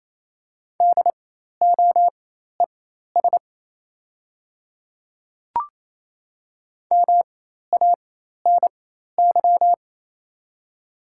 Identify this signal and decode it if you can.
Morse code, heard as a keyed tone, about 28 wpm, spaced as for 13 wpm, 700 Hz. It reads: DOES MANY
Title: 4x Character Spacing - Top 200 Words - 28wpm